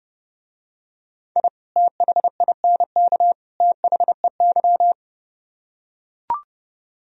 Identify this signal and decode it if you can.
Morse code, heard as a keyed tone, about 30 wpm, 700 Hz. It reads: I THINK THEY